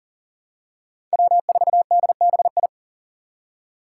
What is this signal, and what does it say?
Morse code, heard as a keyed tone, about 38 wpm, 700 Hz. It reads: W4DBI
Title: Five Challenging Letter-Number Combinations 40wpm